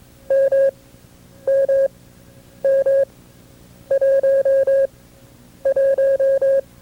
Morse code, heard as a keyed tone, about 22 wpm, spaced as for 9 wpm, 540 Hz. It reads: MMM11
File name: FreeFiles / CW-practice / KB6MT-tape2-side2